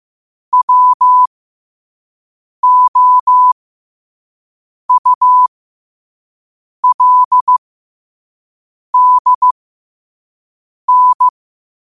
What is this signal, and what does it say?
Morse code, heard as a keyed tone, about 15 wpm, spaced as for 5 wpm, 1000 Hz. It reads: WOULDN
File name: qotdfiles / 5wpm